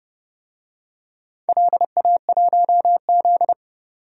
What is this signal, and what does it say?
Morse code, heard as a keyed tone, about 30 wpm, 700 Hz. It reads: LA1Z
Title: DXCC-100 Call Signs - 4 Characters - 30wpm